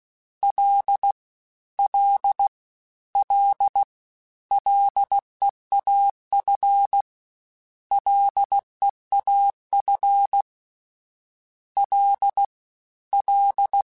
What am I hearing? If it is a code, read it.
Morse code, heard as a keyed tone, about 16 wpm, 800 Hz. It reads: L L L LEAF LEAF L L
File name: Combinations / Combinations-mp3s / Combinations@15